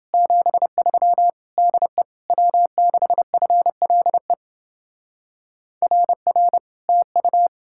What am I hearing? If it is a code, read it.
Morse code, heard as a keyed tone, about 29 wpm, 700 Hz. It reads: 73 DE W6FLE RR TU